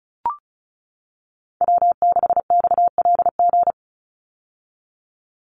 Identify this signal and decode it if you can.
Morse code, heard as a keyed tone, about 34 wpm, 700 Hz. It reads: W6XLG
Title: USA Call Signs 35wpm